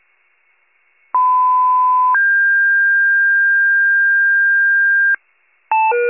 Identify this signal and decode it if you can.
Two-tone page, heard as a one-second tone, then a three-second tone, about 1010 then 1600 Hz. It is heard from 1.1 to 5.2 s.
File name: wcalert